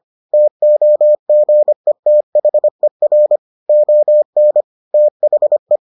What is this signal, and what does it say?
Morse code, heard as a keyed tone, about 25 wpm, 600 Hz. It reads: TOGETHER ON THE